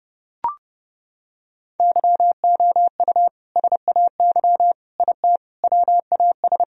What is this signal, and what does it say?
Morse code, heard as a keyed tone, about 30 wpm, 700 Hz. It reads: YOU SAY IT WAS
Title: Sentences from Top 1000 Words - Review Part 1 - 30wpm